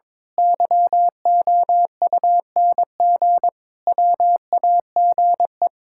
Morse code, heard as a keyed tone, about 21 wpm, 700 Hz. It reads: YOUNG WAGE
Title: Pangram Challenge v2 - 22wpm